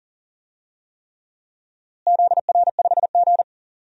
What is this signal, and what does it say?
Morse code, heard as a keyed tone, about 40 wpm, 700 Hz. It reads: ZR5G
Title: DXCC-100 Call Signs - 4 Characters - Repeated 3x - 40wpm